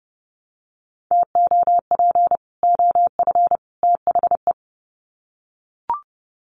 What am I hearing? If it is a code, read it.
Morse code, heard as a keyed tone, about 30 wpm, 700 Hz. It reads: TOP OF THE
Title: Sets of 3 Words 30wpm